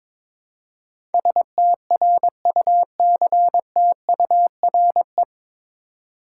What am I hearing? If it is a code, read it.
Morse code, heard as a keyed tone, about 22 wpm, 700 Hz. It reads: STRUCTURE
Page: Morse Code Ninja practice 22wpm